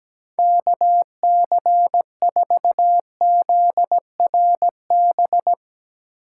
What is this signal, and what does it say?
Morse code, heard as a keyed tone, about 18 wpm, 700 Hz. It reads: KC4ZRB
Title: Speed Racing - USA 2x3 Call Signs 17wpm